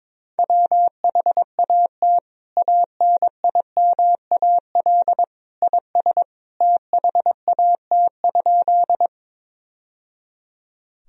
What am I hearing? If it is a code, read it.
Morse code, heard as a keyed tone, about 22 wpm, 700 Hz. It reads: WHAT ANIMAL IS THAT?
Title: Sentences from Top 1000 Words - Review Part 1 - 22wpm